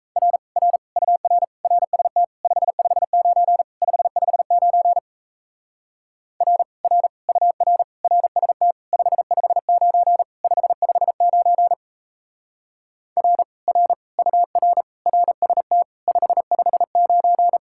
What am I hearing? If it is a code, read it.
Morse code, heard as a keyed tone, about 36 wpm, 700 Hz. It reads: R R UR RST 559 559 R R UR RST 559 559 R R UR RST 559